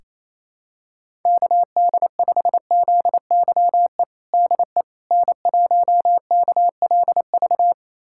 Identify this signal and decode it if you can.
Morse code, heard as a keyed tone, about 27 wpm, 700 Hz. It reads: KD5ZYE DE N1KLV
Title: USA Callsign DE Callsign 28wpm